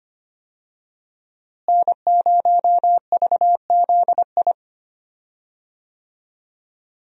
Morse code, heard as a keyed tone, about 25 wpm, 700 Hz. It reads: N0VZI
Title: USA Call Signs - 5 Characters - Repeated 2x - 25wpm